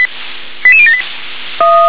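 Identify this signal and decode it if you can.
Five-tone selective call (ZVEI), heard as quick runs of five tones, tones of about 70 ms. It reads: 88078
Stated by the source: ZVEI five-tone call received by SDR.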